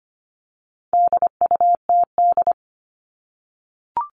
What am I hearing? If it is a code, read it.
Morse code, heard as a keyed tone, about 25 wpm, 700 Hz. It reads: DUTD
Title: All Cut Numbers - 4-Digit 25wpm